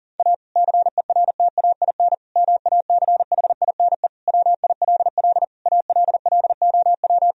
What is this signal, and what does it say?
Morse code, heard as a keyed tone, about 40 wpm, 700 Hz. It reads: A CERTAIN MACHINE WILL ALLOW
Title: Sentences from Top 1000 Words - Review Part 2 - 40wpm